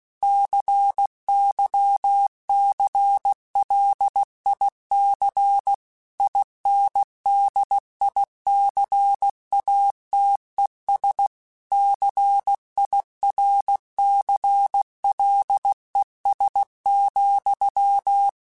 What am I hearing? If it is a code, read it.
Morse code, heard as a keyed tone, about 16 wpm, 800 Hz. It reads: CYCLIC INDICATES CIRCLES,
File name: Technical Subjects / Technical Subjects-mp3s / XL @15